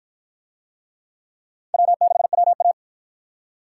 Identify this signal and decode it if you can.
Morse code, heard as a keyed tone, about 47 wpm, 700 Hz. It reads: W6WA